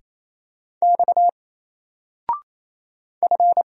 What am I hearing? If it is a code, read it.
Morse code, heard as a keyed tone, about 29 wpm, 700 Hz. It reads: X F
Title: Single Character (Letter:Number:Punctuation:Prosign) 28wpm